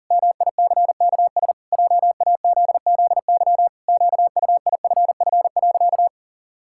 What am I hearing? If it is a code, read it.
Morse code, heard as a keyed tone, about 40 wpm, 700 Hz. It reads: MICKS JAZZY QUIFF.
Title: Pangram Challenge v2 - 40wpm